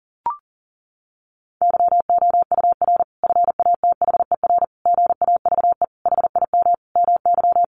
Morse code, heard as a keyed tone, about 40 wpm, 700 Hz. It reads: YOUR FATHER GAVE HIM MY
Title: Sentences from Top 1400 Words 40wpm